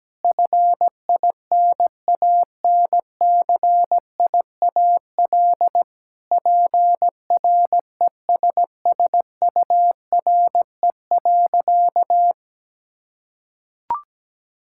Morse code, heard as a keyed tone, about 17 wpm, 700 Hz. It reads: FINANCIAL PRESSURE.
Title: Sentences from Top 1000 Words - Review Part 6 - 17wpm